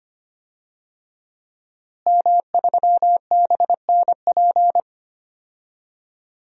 Morse code, heard as a keyed tone, about 25 wpm, 700 Hz. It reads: M3BNP
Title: DXCC-100 Call Signs - 5 Characters - Repeated 3x - 25wpm